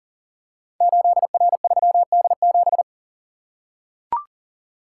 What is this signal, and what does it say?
Morse code, heard as a keyed tone, about 40 wpm, 700 Hz. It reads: 8R3D7
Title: Code Groups - 5 Letter:Numbers - 40wpm